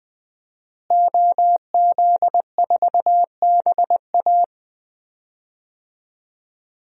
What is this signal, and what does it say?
Morse code, heard as a keyed tone, about 20 wpm, 700 Hz. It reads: OZ4BA
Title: DXCC-100 Call Signs - 5 Characters - Repeated 3x - 20wpm